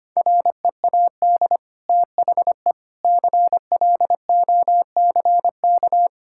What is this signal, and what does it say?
Morse code, heard as a keyed tone, about 25 wpm, 690 Hz. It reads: READ THE CLOCK
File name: Easy Sentences with Hints - 25wpm